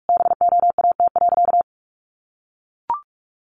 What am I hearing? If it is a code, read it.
Morse code, heard as a keyed tone, about 44 wpm, 700 Hz. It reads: BOAT.
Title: Sentences from Top 1400 Words 45wpm